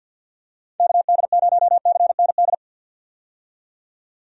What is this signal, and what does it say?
Morse code, heard as a keyed tone, about 50 wpm, 700 Hz. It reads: KD0CND
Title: USA Call Signs - 6 Characters - 50wpm